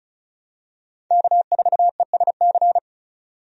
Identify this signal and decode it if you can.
Morse code, heard as a keyed tone, about 36 wpm, 700 Hz. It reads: K4ESC
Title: USA Call Signs - 5 Characters - 35wpm